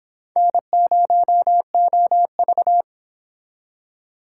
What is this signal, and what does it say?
Morse code, heard as a keyed tone, about 30 wpm, 700 Hz. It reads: N0OV